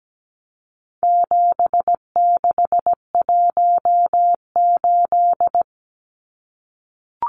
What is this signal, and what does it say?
Morse code, heard as a keyed tone, about 17 wpm, 700 Hz. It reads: 7618
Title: Numbers - 4-Digit 17wpm